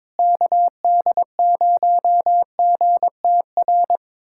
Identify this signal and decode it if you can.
Morse code, heard as a keyed tone, about 22 wpm, 700 Hz. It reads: KD0GTR
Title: USA Callsign DE Callsign - 6 Characters - 22wpm